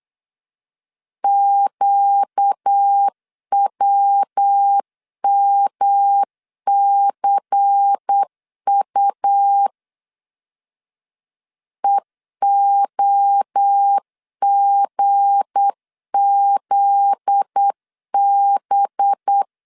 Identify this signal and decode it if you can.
Morse code, heard as a keyed tone, about 8 wpm, 800 Hz. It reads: QWMCU EOGZB
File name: LEZIONI / 8 YJ45 / 8 YJ 8wpm 3313